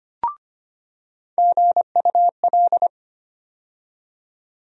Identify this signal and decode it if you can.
Morse code, heard as a keyed tone, about 25 wpm, 700 Hz. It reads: GUL
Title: USA Call Sign Postfixes - 3 Letters 25wpm